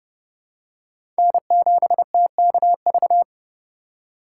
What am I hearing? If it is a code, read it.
Morse code, heard as a keyed tone, about 30 wpm, 700 Hz. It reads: N7TKV
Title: USA Call Signs - 5 Characters - 2x Character Spacing - 30wpm